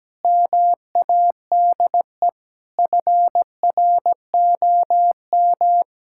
Morse code, heard as a keyed tone, about 17 wpm, 700 Hz. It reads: MADE FROM
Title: Sentences from Top 1000 Words - Review Part 6 - 17wpm